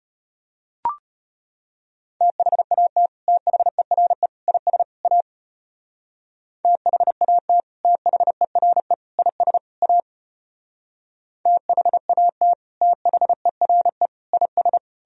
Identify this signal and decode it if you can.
Morse code, heard as a keyed tone, about 34 wpm, 700 Hz. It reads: THAT THERE IS A THAT THERE IS A THAT THERE IS